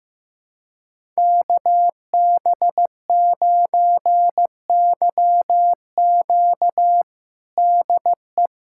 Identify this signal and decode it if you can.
Morse code, heard as a keyed tone, about 15 wpm, 700 Hz. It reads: KB9YQ DE
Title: USA Callsign DE Callsign 15wpm